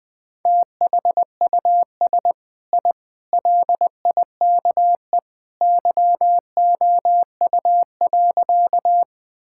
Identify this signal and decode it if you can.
Morse code, heard as a keyed tone, about 20 wpm, 700 Hz. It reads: THUS I LIKE YOU.
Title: Sentences from Top 1000 Words - Review Part 5 - 20wpm